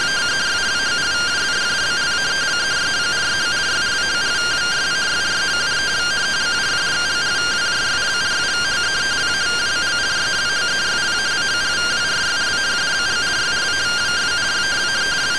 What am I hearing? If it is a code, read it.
Two-tone teleprinter signal, about 170 Hz shift.
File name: findings / 81 axial iodine 262.gen-14 rtty rx